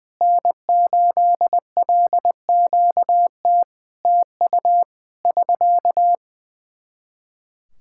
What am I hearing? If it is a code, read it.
Morse code, heard as a keyed tone, about 20 wpm, 700 Hz. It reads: N8LQT TU <SK>
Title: Practice QSO - Exchange 4 - Ending - 20wpm